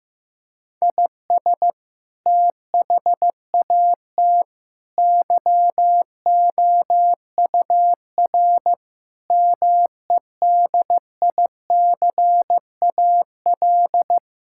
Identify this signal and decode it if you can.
Morse code, heard as a keyed tone, about 16 wpm, 700 Hz. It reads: IS THAT YOUR MEDICAL
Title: Sentences from Top 1000 Words - Review Part 5 - 15wpm